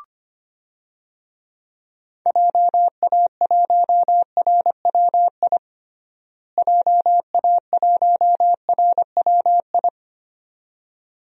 Morse code, heard as a keyed tone, about 25 wpm, 700 Hz. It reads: JA1RWI JA1RWI
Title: USA Call Signs - 6 Characters - Repeated 2x - 25wpm